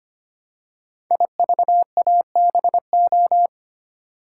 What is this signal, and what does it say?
Morse code, heard as a keyed tone, about 25 wpm, 700 Hz. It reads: IVABO
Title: Code Groups - 5 Letters - 25wpm